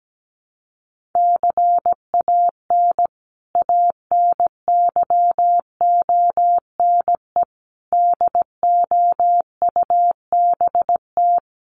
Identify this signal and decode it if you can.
Morse code, heard as a keyed tone, about 17 wpm, 700 Hz. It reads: CAN ANYONE DOUBT
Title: Sentences from Top 1500 Words 17wpm